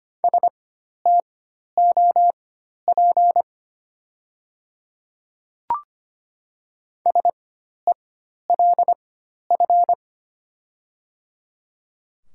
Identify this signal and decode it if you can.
Morse code, heard as a keyed tone, about 25 wpm, spaced as for 12 wpm, 700 Hz. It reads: STOP SELF
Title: 4x Character Spacing - Top 300 Words - 25wpm